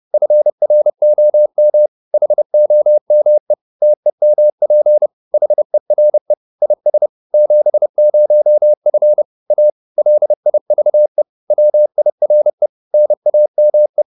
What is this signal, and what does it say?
Morse code, heard as a keyed tone, about 30 wpm, 600 Hz. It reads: FROM HOME TEMP HERE IS 70F A LIVE WIRE NAME